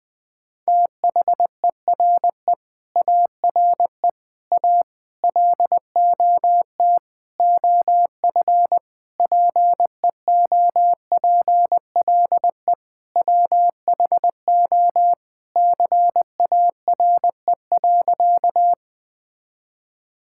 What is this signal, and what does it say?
Morse code, heard as a keyed tone, about 21 wpm, 700 Hz. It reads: THERE ARE A LOT OF PEOPLE WHO CARE.
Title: Sentences with 5-Word Phrases - Spoken Beginning - 20wpm